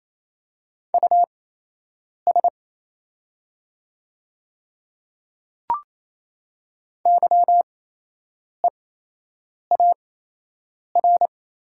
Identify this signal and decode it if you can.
Morse code, heard as a keyed tone, about 28 wpm, spaced as for 8 wpm, 700 Hz. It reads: US YEAR
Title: 8x Character Spacing - Top 200 Words - 28wpm